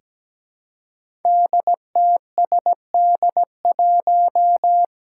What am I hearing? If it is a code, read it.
Morse code, heard as a keyed tone, about 17 wpm, 700 Hz. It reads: DTSD1
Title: Five Challenging Letter-Number Combinations 17wpm